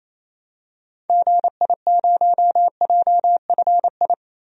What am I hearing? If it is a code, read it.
Morse code, heard as a keyed tone, about 28 wpm, 700 Hz. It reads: GI0JFI
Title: DXCC-100 Call Signs - 6 Characters - Repeated 3x - 28wpm